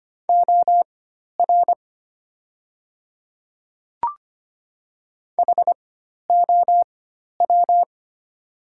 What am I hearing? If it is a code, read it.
Morse code, heard as a keyed tone, about 25 wpm, spaced as for 12 wpm, 700 Hz. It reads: OR HOW
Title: 4x Character Spacing - Top 300 Words - 25wpm